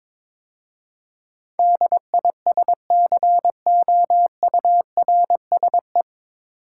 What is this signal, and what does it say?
Morse code, heard as a keyed tone, about 22 wpm, 700 Hz. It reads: DISCOURSE
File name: 9 Letter Words - First 3 Letters Given - 22wpm